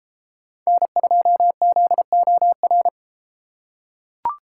Morse code, heard as a keyed tone, about 31 wpm, 700 Hz. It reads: N2ZOR